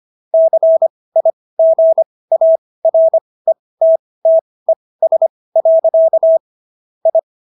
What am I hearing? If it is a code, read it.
Morse code, heard as a keyed tone, about 25 wpm, spaced as for 18 wpm, 650 Hz. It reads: CIGARETTES. I